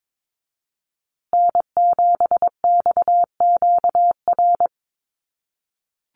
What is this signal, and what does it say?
Morse code, heard as a keyed tone, about 22 wpm, 700 Hz. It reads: N7XQR
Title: USA Call Signs Repeated 2x 22wpm